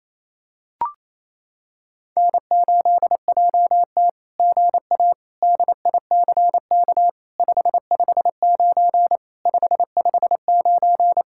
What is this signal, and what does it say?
Morse code, heard as a keyed tone, about 28 wpm, 700 Hz. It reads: N8JT GA DICK 559 559
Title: SOTA QSO Practice 28wpm